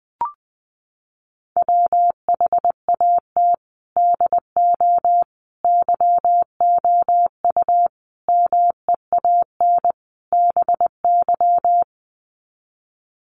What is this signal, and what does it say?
Morse code, heard as a keyed tone, about 20 wpm, 700 Hz. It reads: WHAT DO YOU MEAN BY
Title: Sets of 5 Words 20wpm